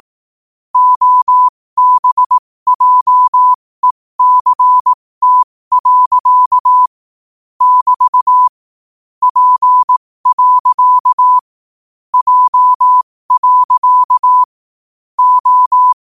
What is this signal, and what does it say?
Morse code, heard as a keyed tone, about 18 wpm, spaced as for 15 wpm, 1000 Hz. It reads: OBJECT. <BT> P. J. O
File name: qotdfiles / 15wpm